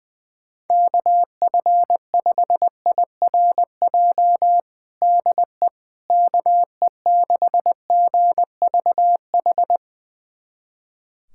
Morse code, heard as a keyed tone, about 20 wpm, 700 Hz. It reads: KF5IRJ DE KE6GVH
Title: USA Callsign DE Callsign - 6 Characters - 20wpm